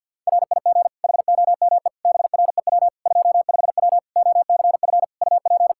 Morse code, heard as a keyed tone, about 50 wpm, 700 Hz. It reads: RIG HOME BREW 25W OCF UP
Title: Practice QSO - Exchange 2 - Info - 50wpm